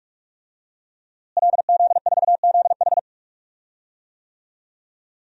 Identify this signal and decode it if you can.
Morse code, heard as a keyed tone, about 43 wpm, 700 Hz. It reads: LZ4ZH
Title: DXCC-100 Call Signs - 5 Characters - Repeated 3x - 45wpm